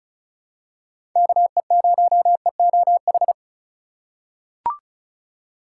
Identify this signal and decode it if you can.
Morse code, heard as a keyed tone, about 36 wpm, 700 Hz. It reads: KE0EOH
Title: USA Call Signs - 6 Characters - 4x Character Spacing - 35wpm